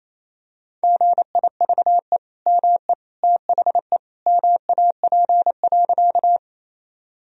Morse code, heard as a keyed tone, about 27 wpm, 700 Hz. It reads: GIVE ME THE MAP.
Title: Sentences from Top 1000 Words - Review Part 2 - 28wpm